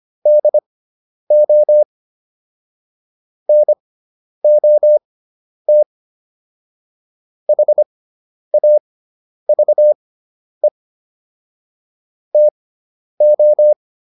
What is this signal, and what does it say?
Morse code, heard as a keyed tone, about 25 wpm, spaced as for 10 wpm, 600 Hz. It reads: DO NOT HAVE TO